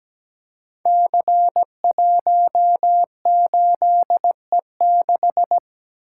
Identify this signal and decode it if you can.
Morse code, heard as a keyed tone, about 17 wpm, 700 Hz. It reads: C18E6
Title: Code Groups - 5 Letter:Numbers - 17wpm